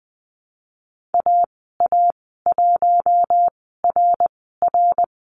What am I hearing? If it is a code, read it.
Morse code, heard as a keyed tone, about 19 wpm, spaced as for 14 wpm, 700 Hz. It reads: AA1RR
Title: 062 Lesson 11g - Letter h - Call Signs 20wpm